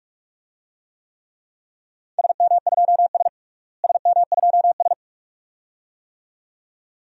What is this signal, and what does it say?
Morse code, heard as a keyed tone, about 42 wpm, 700 Hz. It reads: SM2S SM2S
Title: DXCC-100 Call Signs - 4 Characters - Repeated 2x - 45wpm